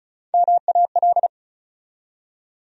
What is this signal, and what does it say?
Morse code, heard as a keyed tone, about 36 wpm, 700 Hz. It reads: MAL
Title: USA Call Sign Postfixes - 3 Letters 35wpm